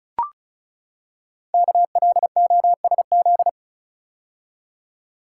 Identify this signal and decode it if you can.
Morse code, heard as a keyed tone, about 36 wpm, 700 Hz. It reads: KLOSZ
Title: Code Groups - 5 Letters - 35wpm